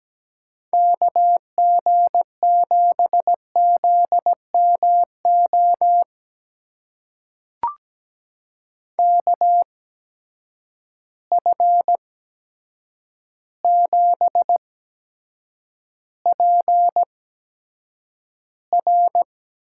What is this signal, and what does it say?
Morse code, heard as a keyed tone, about 17 wpm, spaced as for 8 wpm, 700 Hz. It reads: KG7ZMO KF7PR
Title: USA Call Signs - 6 Characters - 8x Character Spacing - 17wpm